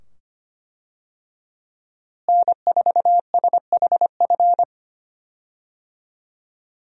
Morse code, heard as a keyed tone, about 25 wpm, 700 Hz. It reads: N4SHF
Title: USA Call Signs - 5 Characters - Repeated 2x - 25wpm